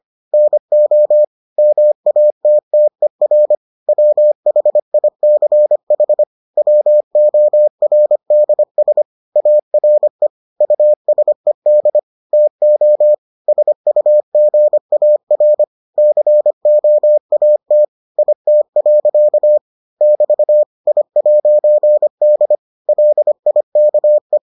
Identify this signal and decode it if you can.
Morse code, heard as a keyed tone, about 25 wpm, 600 Hz. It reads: NO MATTER WHICH WORDS ARE USED TO SUGAR COAT IT. <BT> I'D LIKE